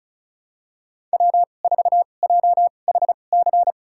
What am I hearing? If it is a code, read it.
Morse code, heard as a keyed tone, about 36 wpm, spaced as for 25 wpm, 700 Hz. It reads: W4JHC
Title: USA Call Signs - 5 Characters - 2x Character Spacing - 35wpm